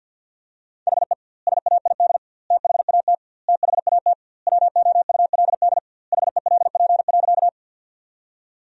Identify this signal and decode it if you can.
Morse code, heard as a keyed tone, about 50 wpm, 700 Hz. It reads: HE SAID THAT THAT WOULD HELP.
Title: 2x Word-Spacing - Sentences from Top 200 Words - 50wpm